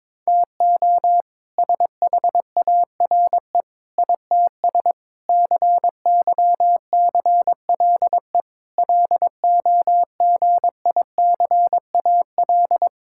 Morse code, heard as a keyed tone, about 22 wpm, 700 Hz. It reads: TO SHARE ITS CYCLE LOGICAL
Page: morse-code-ninja-practice-22wpm